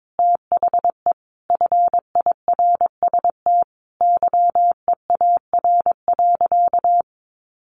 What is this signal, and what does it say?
Morse code, heard as a keyed tone, about 22 wpm, 700 Hz. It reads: THE FIRST YEAR.
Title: Sentences from Top 1300 Words 22wpm